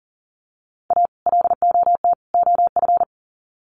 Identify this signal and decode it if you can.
Morse code, heard as a keyed tone, about 40 wpm, 700 Hz. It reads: A LOT OF
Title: Sets of 3 Words 40wpm